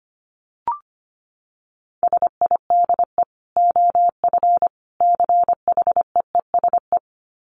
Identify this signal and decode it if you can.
Morse code, heard as a keyed tone, about 25 wpm, 700 Hz. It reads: SIDE OF CHEESE
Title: Dit Challenge 25wpm